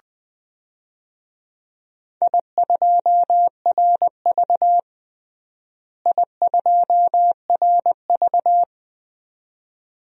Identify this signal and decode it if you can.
Morse code, heard as a keyed tone, about 20 wpm, 700 Hz. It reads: I2RV I2RV
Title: DXCC-100 Call Signs - 4 Characters - Repeated 2x - 20wpm